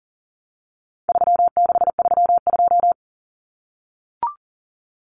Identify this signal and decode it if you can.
Morse code, heard as a keyed tone, about 40 wpm, 700 Hz. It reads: 3632
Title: Common Cut Numbers - 4-Digit 40wpm